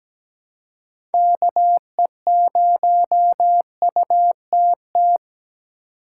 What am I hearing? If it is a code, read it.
Morse code, heard as a keyed tone, about 17 wpm, 700 Hz. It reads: KE0UTT
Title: USA Call Signs - 6 Characters - 6x Character Spacing - 17wpm